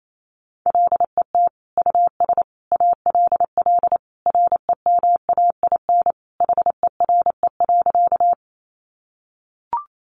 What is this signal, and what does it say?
Morse code, heard as a keyed tone, about 28 wpm, 700 Hz. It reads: LET US ALL REMAIN HERE.
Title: Sentences from Top 1200 Words 28wpm